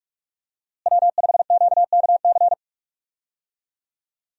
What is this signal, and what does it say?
Morse code, heard as a keyed tone, about 46 wpm, 700 Hz. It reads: W5QKC
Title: USA Call Signs - 5 Characters - Repeated 3x - 45wpm